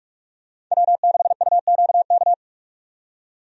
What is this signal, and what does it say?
Morse code, heard as a keyed tone, about 43 wpm, 700 Hz. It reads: W6UQK
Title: USA Call Signs - 5 Characters - 45wpm